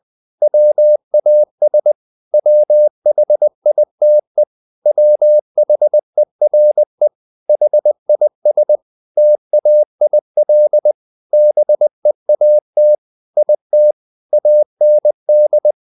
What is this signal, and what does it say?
Morse code, heard as a keyed tone, about 20 wpm, 600 Hz. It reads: WAS WHITE WHERE HIS TAIL BEAT IT AND
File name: the old man and the sea part 2 20wpm